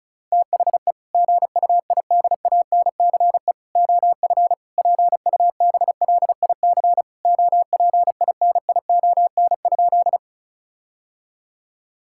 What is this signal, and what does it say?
Morse code, heard as a keyed tone, about 36 wpm, 700 Hz. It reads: THE GUIDANCE OF PUBLIC OPINION?
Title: Sentences from Top 1600 Words 35wpm